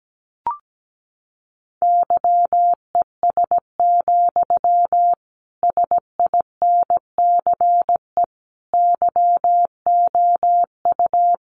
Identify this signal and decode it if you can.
Morse code, heard as a keyed tone, about 17 wpm, 700 Hz. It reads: YES, SINCE YOU
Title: Sentences from Top 1300 Words 17wpm